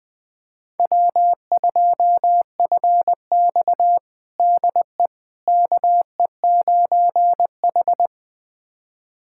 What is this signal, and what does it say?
Morse code, heard as a keyed tone, about 20 wpm, 700 Hz. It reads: W2FX DE KE9H
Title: USA Callsign DE Callsign - 4 Characters - 20wpm